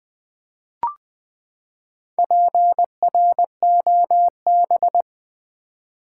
Morse code, heard as a keyed tone, about 20 wpm, 700 Hz. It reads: PROB
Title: Build A Word - 4 Letters At A Time - 11-Letter Words - 20wpm